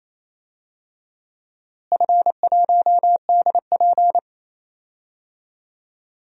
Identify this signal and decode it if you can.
Morse code, heard as a keyed tone, about 28 wpm, 700 Hz. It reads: F1DP
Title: DXCC-100 Call Signs - 4 Characters - Repeated 2x - 28wpm